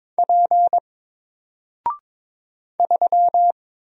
Morse code, heard as a keyed tone, about 22 wpm, 700 Hz. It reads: P 3
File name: Single-Letter-Number-22wpm